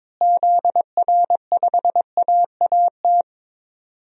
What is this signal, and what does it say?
Morse code, heard as a keyed tone, about 22 wpm, 700 Hz. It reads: ZR5AAT
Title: DXCC-100 Call Signs - 6 Characters - Repeated 2x - 22wpm